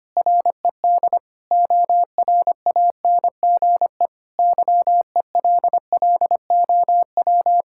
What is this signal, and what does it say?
Morse code, heard as a keyed tone, about 25 wpm, 700 Hz. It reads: RED ORANGE YELLOW
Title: Pangram Challenge v2 - 25wpm